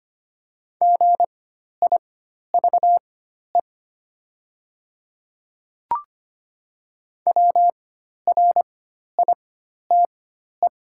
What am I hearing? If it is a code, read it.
Morse code, heard as a keyed tone, about 25 wpm, spaced as for 12 wpm, 690 Hz. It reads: GIVE WRITE
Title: 4x Character Spacing - Top 200 Words - 25wpm